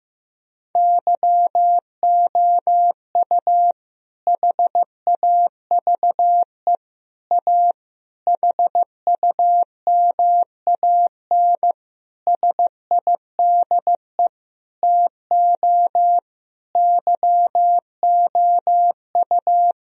Morse code, heard as a keyed tone, about 15 wpm, 700 Hz. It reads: YOU HAVE A HUMAN SIDE TO YOU
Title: Sentences from Top 1000 Words - Review Part 3 - 15wpm